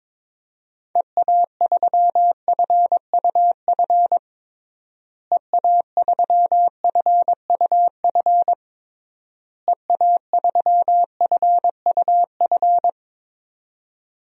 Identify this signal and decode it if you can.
Morse code, heard as a keyed tone, about 22 wpm, 700 Hz. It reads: EA3FUF EA3FUF EA3FUF
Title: DXCC-100 Call Signs - 6 Characters - Repeated 3x - 22wpm